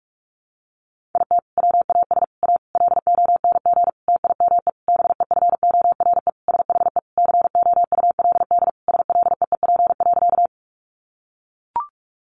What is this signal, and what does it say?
Morse code, heard as a keyed tone, about 44 wpm, 700 Hz. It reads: IT WAS A LONG TIME BEFORE SHE COULD SLEEP.
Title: Sentences from Top 500 Words 45wpm